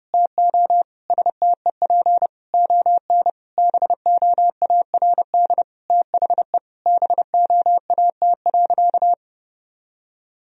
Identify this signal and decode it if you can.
Morse code, heard as a keyed tone, about 30 wpm, 700 Hz. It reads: TO STEP ON BOARD THE BOAT.
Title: Sentences from Top 1000 Words - Review Part 4 - 30wpm